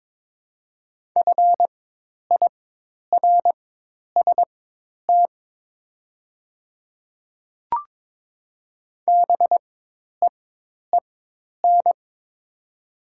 Morse code, heard as a keyed tone, about 22 wpm, spaced as for 10 wpm, 700 Hz. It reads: FIRST BEEN
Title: 4x Character Spacing - Top 100 Words - 22wpm